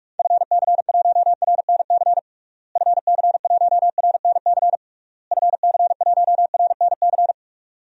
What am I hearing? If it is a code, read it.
Morse code, heard as a keyed tone, about 46 wpm, 700 Hz. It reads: FC1RNC FC1RNC FC1RNC
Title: DXCC-100 Call Signs - 6 Characters - Repeated 3x - 45wpm